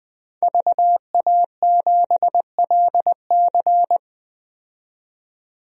Morse code, heard as a keyed tone, about 20 wpm, 700 Hz. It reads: VA7LC
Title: DXCC-100 Call Signs - 5 Characters - 20wpm